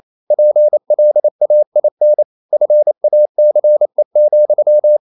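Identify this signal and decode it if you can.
Morse code, heard as a keyed tone, about 28 wpm, 600 Hz. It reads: PLAIN FACE,
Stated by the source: Morse code practice at 28 wpm, text of Ken Follett's novel Never in English, from CW med Gnister.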